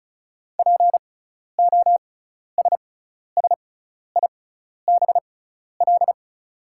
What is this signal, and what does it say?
Morse code, heard as a keyed tone, about 36 wpm, spaced as for 12 wpm, 690 Hz. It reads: POSSIBL
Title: 6x Character Spacing - Top 500 Words - 35wpm